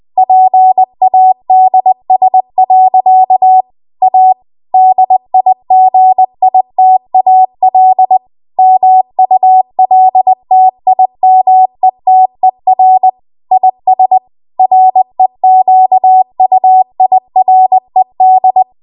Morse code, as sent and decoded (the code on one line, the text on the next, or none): .--. .- -.. ... .-.-.- / .- / -.. .. --. .. - .- .-.. / -- ..- .-.. - .. -- . - . .-. / .. ... / .-. --.- ..- .. .-. -..
PADS. A DIGITAL MULTIMETER IS REQUIRED